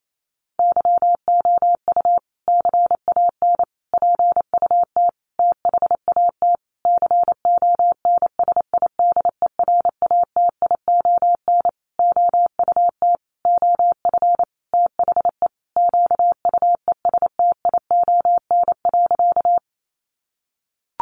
Morse code, heard as a keyed tone, about 28 wpm, 700 Hz. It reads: YOU CAN PUT THAT CONSIDERATION OUT OF THE QUESTION.
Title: Sentences from Top 1500 Words 28wpm